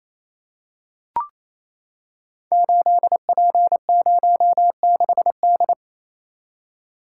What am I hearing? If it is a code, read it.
Morse code, heard as a keyed tone, about 27 wpm, 700 Hz. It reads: 8P06D
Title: Code Groups - 5 Letter:Numbers - 28wpm